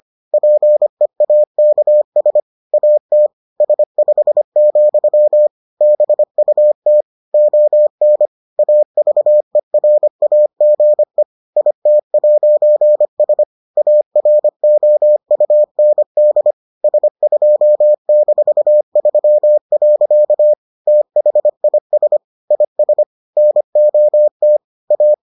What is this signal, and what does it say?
Morse code, heard as a keyed tone, about 25 wpm, 600 Hz. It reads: PEAKS AT S5, BUT ON AVERAGE IT'S AROUND S2-3. THIS IS NOT A